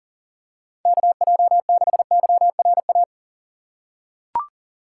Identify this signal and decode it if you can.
Morse code, heard as a keyed tone, about 40 wpm, 700 Hz. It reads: KJ6YRA
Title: USA Call Signs - 6 Characters - 3x Character Spacing - 40wpm